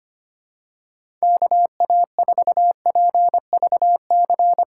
Morse code, heard as a keyed tone, about 25 wpm, 700 Hz. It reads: KA4PVC
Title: USA Call Signs - 6 Characters - 2x Character Spacing - 25wpm